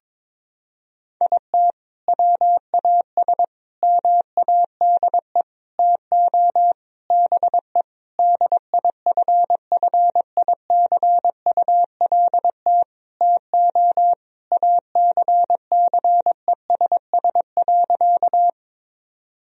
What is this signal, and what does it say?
Morse code, heard as a keyed tone, about 22 wpm, 700 Hz. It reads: IT WAS MADE TO BE DIFFICULT TO ACCESS.
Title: Sentences from Top 1000 Words - Review Part 4 - 22wpm